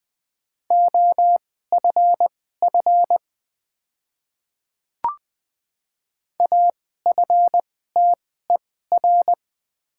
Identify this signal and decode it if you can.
Morse code, heard as a keyed tone, about 20 wpm, spaced as for 15 wpm, 700 Hz. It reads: OFF AFTER
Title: 2x Character Spacing - Top 500 Words - 20wpm